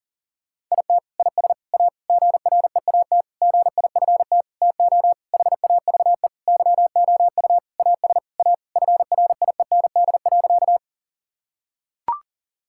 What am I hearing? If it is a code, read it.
Morse code, heard as a keyed tone, about 40 wpm, 700 Hz. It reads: IT IS A GREAT GIFT TO HAVE YOU AS A FRIEND.
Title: Sentences from Top 1000 Words - Review Part 6 - 40wpm